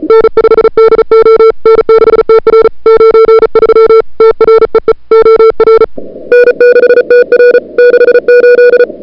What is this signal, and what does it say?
Morse code, heard as a keyed tone, about 34 wpm, 440 Hz. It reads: N5DO N6TR 93 TREE OR N6TR 68
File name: CWSprint / February 2007 CW Sprint / 01-2007 CW Sprint Hour1